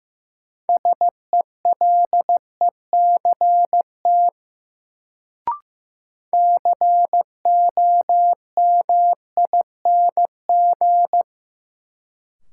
Morse code, heard as a keyed tone, about 16 wpm, 700 Hz. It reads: SELECT COMING
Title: Sentences from Top 1300 Words - New Word Drill - 15wpm